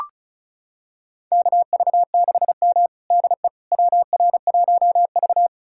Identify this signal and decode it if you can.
Morse code, heard as a keyed tone, about 36 wpm, 700 Hz. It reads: KV6M DE WR1V
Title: USA Callsign DE Callsign - 4 Characters - 35wpm